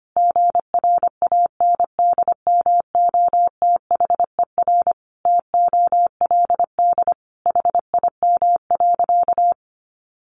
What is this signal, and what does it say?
Morse code, heard as a keyed tone, about 25 wpm, 700 Hz. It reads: GRANDMOTHER TOLD HIM.
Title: Sentences from Top 1200 Words 25wpm